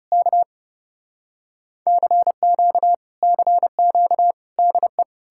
Morse code, heard as a keyed tone, about 37 wpm, 700 Hz. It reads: K CQ CQ DE
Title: Speed-Racing Common QSO Phrases - 30wpm